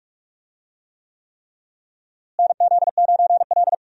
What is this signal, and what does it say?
Morse code, heard as a keyed tone, about 41 wpm, 700 Hz. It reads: NZ9L
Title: USA Call Signs - 4 Characters - Repeated 3x - 45wpm